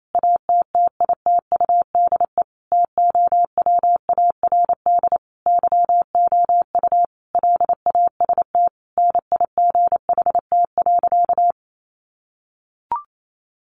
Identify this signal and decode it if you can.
Morse code, heard as a keyed tone, about 28 wpm, 700 Hz. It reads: ATTITUDE TOWARD YOU LAST NIGHT.
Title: Sentences from Top 1300 Words 28wpm